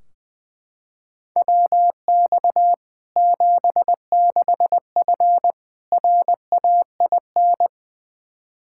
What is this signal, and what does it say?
Morse code, heard as a keyed tone, about 20 wpm, 700 Hz. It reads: WX 76F RAIN
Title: Practice QSO - Exchange 2 - Info - 20wpm